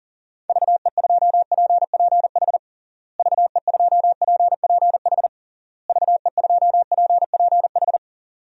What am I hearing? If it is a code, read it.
Morse code, heard as a keyed tone, about 40 wpm, 700 Hz. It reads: VE2PPH VE2PPH VE2PPH
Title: DXCC-100 Call Signs - 6 Characters - Repeated 3x - 40wpm